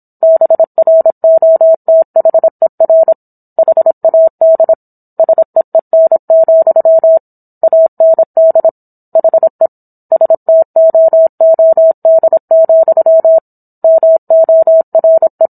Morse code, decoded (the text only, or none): BROTHER HAD SEEN, AND HE STOOD, MORE